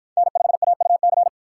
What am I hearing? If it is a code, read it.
Morse code, heard as a keyed tone, about 51 wpm, 700 Hz. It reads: N5AUC